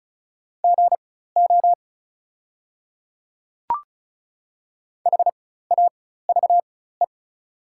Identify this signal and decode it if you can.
Morse code, heard as a keyed tone, about 36 wpm, spaced as for 16 wpm, 690 Hz. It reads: GO HAVE
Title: 4x Character Spacing - Top 200 Words - 35wpm